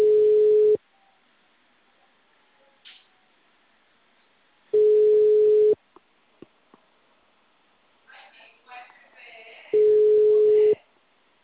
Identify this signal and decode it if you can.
Ringback tone (Europe, 425 Hz).
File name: out- -2011- - - .3059